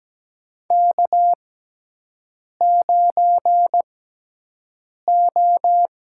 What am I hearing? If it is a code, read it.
Morse code, heard as a keyed tone, about 17 wpm, spaced as for 6 wpm, 700 Hz. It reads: K9O
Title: USA Call Signs - 6x Character Spacing - 17wpm